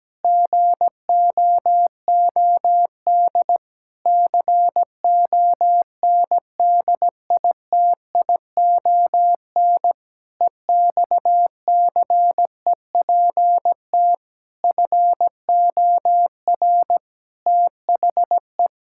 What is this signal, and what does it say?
Morse code, heard as a keyed tone, about 17 wpm, 700 Hz. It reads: GOOD CONDITION EXCEPT FOR THE
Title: Pangram Challenge v2 - 17wpm